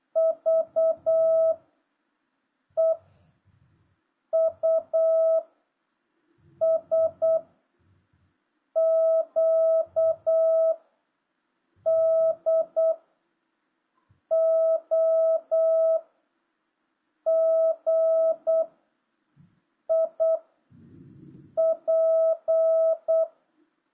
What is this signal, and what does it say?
Morse code, heard as a keyed tone, about 8 wpm, 650 Hz. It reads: V E U S Q D O G I P